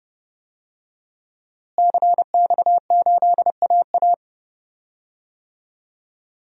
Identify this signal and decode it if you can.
Morse code, heard as a keyed tone, about 30 wpm, 700 Hz. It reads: CX8AA
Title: DXCC-100 Call Signs - 5 Characters - Repeated 3x - 30wpm